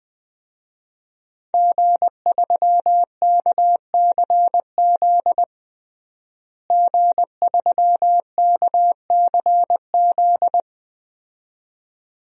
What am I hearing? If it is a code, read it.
Morse code, heard as a keyed tone, about 20 wpm, 700 Hz. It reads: G3KCZ G3KCZ
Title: DXCC-100 Call Signs - 5 Characters - Repeated 2x - 20wpm